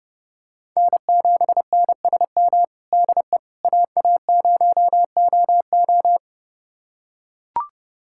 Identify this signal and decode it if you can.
Morse code, heard as a keyed tone, about 32 wpm, 700 Hz. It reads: N7NSM DE AA0OO
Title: USA Callsign DE Callsign - 5 Characters - 30wpm